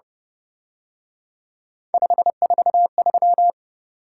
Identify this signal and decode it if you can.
Morse code, heard as a keyed tone, about 31 wpm, 700 Hz. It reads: 543
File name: Speed-Racing - Three Letter-Number Combinations - 25wpm